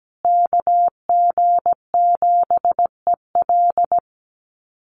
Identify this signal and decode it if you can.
Morse code, heard as a keyed tone, about 17 wpm, 700 Hz. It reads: KG7EL
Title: USA Call Signs Repeated 3x 17wpm